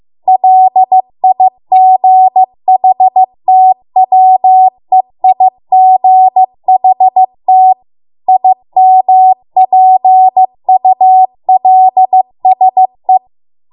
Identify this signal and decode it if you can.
Morse code, heard as a keyed tone, about 15 wpm, 750 Hz. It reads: LIGHTWEIGHT IMPULSE